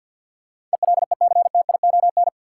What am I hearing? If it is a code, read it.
Morse code, heard as a keyed tone, about 53 wpm, 700 Hz. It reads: ELECTION
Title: Sentences from Top 1500 Words - New Word Drill - 50wpm